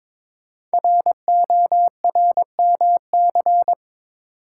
Morse code, heard as a keyed tone, about 22 wpm, 700 Hz. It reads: RORMC
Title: Code Groups - 5 Letters - 22wpm